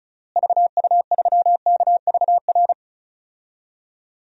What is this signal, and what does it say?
Morse code, heard as a keyed tone, about 34 wpm, 700 Hz. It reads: VU3KVR
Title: DXCC-100 Call Signs - 6 Characters - Repeated 3x - 35wpm